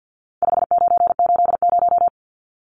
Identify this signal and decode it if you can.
Morse code, heard as a keyed tone, about 54 wpm, 700 Hz. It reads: 5980
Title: Numbers - 4-Digit 50wpm